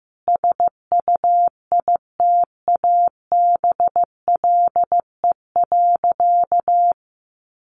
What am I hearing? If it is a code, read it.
Morse code, heard as a keyed tone, about 14 wpm, 700 Hz. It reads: SUITABLE.
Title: Sentences from Top 1400 Words 15wpm